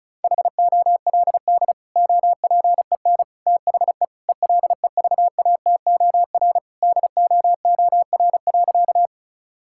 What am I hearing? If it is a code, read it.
Morse code, heard as a keyed tone, about 36 wpm, 700 Hz. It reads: HOLD OPEN THE ELEVATOR DOOR.
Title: Sentences from Top 1600 Words 35wpm